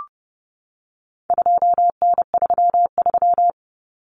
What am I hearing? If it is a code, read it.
Morse code, heard as a keyed tone, about 30 wpm, 700 Hz. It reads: 2N33
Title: Common Cut Numbers - 4-Digit 30wpm